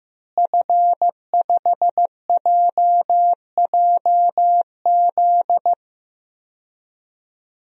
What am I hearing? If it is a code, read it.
Morse code, heard as a keyed tone, about 15 wpm, 700 Hz. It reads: F5JJZ
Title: DXCC-100 Call Signs - 5 Characters - Repeated 3x - 15wpm